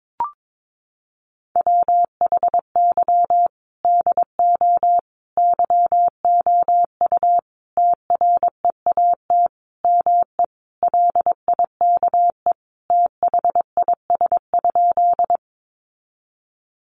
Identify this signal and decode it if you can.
Morse code, heard as a keyed tone, about 22 wpm, 700 Hz. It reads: WHY DO YOU TREAT ME LIKE THIS?
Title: Sentences from Top 1300 Words 22wpm